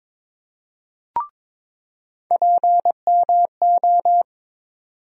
Morse code, heard as a keyed tone, about 22 wpm, 700 Hz. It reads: PMO
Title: USA Call Sign Postfixes - 3 Letters 22wpm